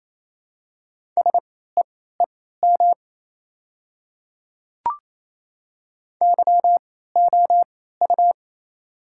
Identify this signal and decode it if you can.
Morse code, heard as a keyed tone, about 28 wpm, spaced as for 16 wpm, 700 Hz. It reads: SEEM YOU
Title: 3x Character Spacing - Top 300 Words - 28wpm